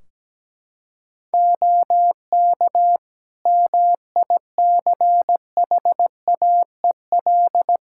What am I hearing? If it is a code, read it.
Morse code, heard as a keyed tone, about 17 wpm, 700 Hz. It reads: OK MICHAEL
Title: Practice QSO - Exchange 4 - Ending - 17wpm